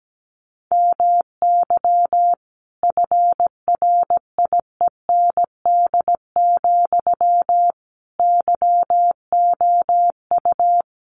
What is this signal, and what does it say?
Morse code, heard as a keyed tone, about 17 wpm, 700 Hz. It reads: MY FRIEND, YOU
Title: Sentences from Top 1300 Words 17wpm